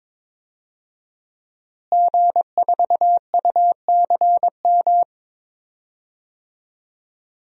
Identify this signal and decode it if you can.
Morse code, heard as a keyed tone, about 22 wpm, 700 Hz. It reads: G4UCM
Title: DXCC-100 Call Signs - 5 Characters - Repeated 2x - 22wpm